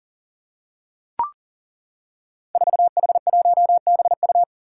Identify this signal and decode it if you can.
Morse code, heard as a keyed tone, about 40 wpm, 700 Hz. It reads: 4H1BU
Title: Five Challenging Letter-Number Combinations 40wpm